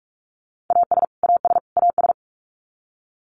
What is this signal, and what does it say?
Morse code, heard as a keyed tone, about 44 wpm, 700 Hz. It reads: AS AS AS
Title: Top 200 Words Repeated 45wpm